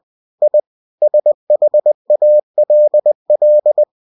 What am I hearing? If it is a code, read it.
Morse code, heard as a keyed tone, about 20 wpm, 600 Hz. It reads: I SHALL
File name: cw-books / winnie-the-pooh / winnie-20wpm-page-0016